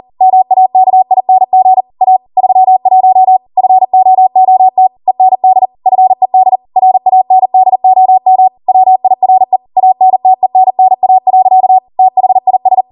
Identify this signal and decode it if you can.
Morse code, heard as a keyed tone, about 40 wpm, 750 Hz. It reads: MAKING A 31 FOOT END FED RANDOM WIRE ANTENNA. THIS